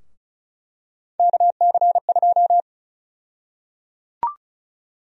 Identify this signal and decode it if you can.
Morse code, heard as a keyed tone, about 36 wpm, 700 Hz. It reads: KC2
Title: USA Call Sign Prefixes - Repeated 2x - 35wpm